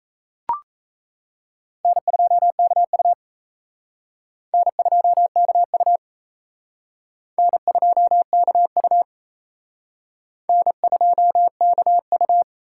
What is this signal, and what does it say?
Morse code, heard as a keyed tone, about 35 wpm, 700 Hz. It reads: N2KU N2KU N2KU N2KU
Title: Speed Racing - USA 1x2 Call Signs 28wpm